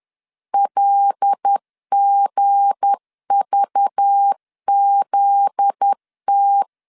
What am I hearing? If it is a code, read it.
Morse code, heard as a keyed tone, about 10 wpm, 800 Hz. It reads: LGVZT
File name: LEZIONI / 10 SIGNS / 10B INT 10wpm 2844